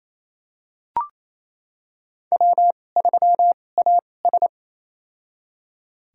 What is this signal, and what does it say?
Morse code, heard as a keyed tone, about 28 wpm, spaced as for 20 wpm, 700 Hz. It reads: W3AS
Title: USA Call Signs - 4 Characters - 2x Character Spacing - 28wpm